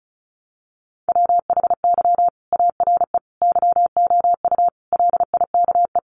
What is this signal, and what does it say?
Morse code, heard as a keyed tone, about 34 wpm, 700 Hz. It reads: WHY ARE YOU LIKE
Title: Sentences from Top 200 Words 35wpm